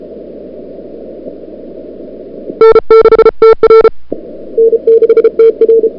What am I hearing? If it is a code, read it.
Morse code, heard as a keyed tone, about 34 wpm, 440 Hz. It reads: N6TR N6TR
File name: n6tr / Fab2007CWSprintHour1